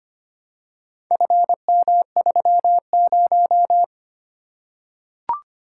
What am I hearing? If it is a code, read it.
Morse code, heard as a keyed tone, about 25 wpm, 700 Hz. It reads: FM30
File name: practice / 25wpm / Four Letter-Number Combinations - 25wpm